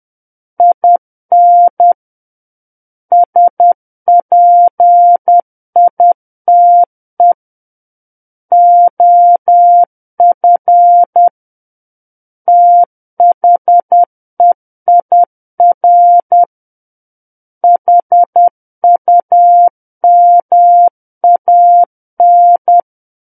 Morse code, heard as a keyed tone, about 10 wpm, 700 Hz. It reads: IN SPITE OF THEIR HUMAN
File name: Book One / 01-Chapter 1 - 10 WPM